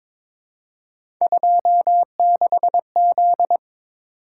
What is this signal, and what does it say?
Morse code, heard as a keyed tone, about 23 wpm, 700 Hz. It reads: 26Z
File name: practice / 22wpm / Three Letter-Number Combinations - 22wpm